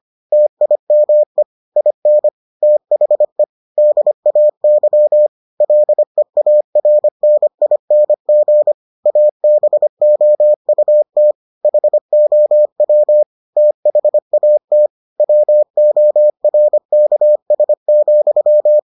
Morse code, heard as a keyed tone, about 25 wpm, 600 Hz. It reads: TIME IN THE DAY LEARNING ABOUT HOW THAT WORKS,